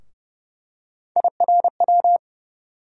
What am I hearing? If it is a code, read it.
Morse code, heard as a keyed tone, about 32 wpm, 700 Hz. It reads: IRW
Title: USA Call Sign Postfixes - 3 Letters 30wpm